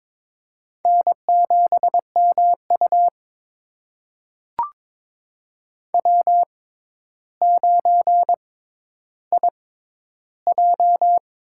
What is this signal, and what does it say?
Morse code, heard as a keyed tone, about 22 wpm, spaced as for 11 wpm, 700 Hz. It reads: N7MU W9IJ
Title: USA Call Signs - 4 Characters - 6x Character Spacing - 22wpm